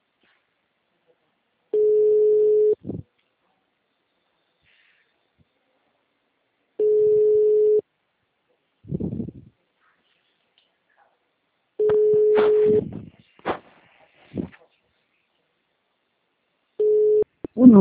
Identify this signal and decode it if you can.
Ringback tone (Europe, 425 Hz).